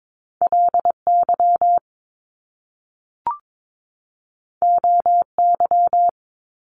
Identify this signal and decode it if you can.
Morse code, heard as a keyed tone, about 22 wpm, 700 Hz. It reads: LY OY
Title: Top Two Letter Combinations in Words 22wpm